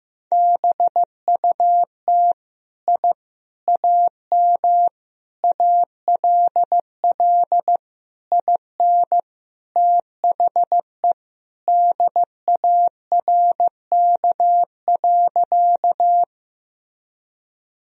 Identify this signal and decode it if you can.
Morse code, heard as a keyed tone, about 15 wpm, 700 Hz. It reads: BUT I AM ALL IN THE DARK.
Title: Sentences from Top 1000 Words - Review Part 2 - 15wpm